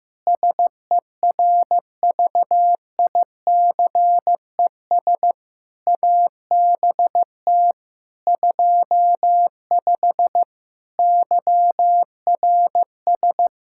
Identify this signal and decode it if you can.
Morse code, heard as a keyed tone, about 15 wpm, 700 Hz. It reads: SERVICES ABT 25 YRS
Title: Practice QSO - Exchange 3 - More Info - 15wpm